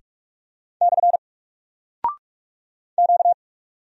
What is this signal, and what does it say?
Morse code, heard as a keyed tone, about 44 wpm, 700 Hz. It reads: / Q
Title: Single Character (Letter:Number:Punctuation:Prosign) 45wpm